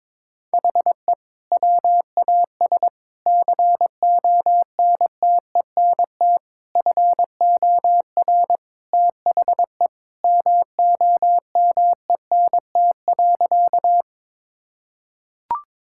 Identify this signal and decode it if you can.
Morse code, heard as a keyed tone, about 22 wpm, 700 Hz. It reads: HE WAS CONTENT FOR THE MOMENT.
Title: Sentences from Top 1000 Words - Review Part 6 - 22wpm